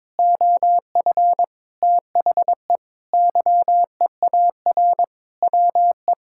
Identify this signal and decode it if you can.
Morse code, heard as a keyed tone, about 22 wpm, 700 Hz. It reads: OF THE YEAR WE
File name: Sentences with 7-Word Phrases - 22wpm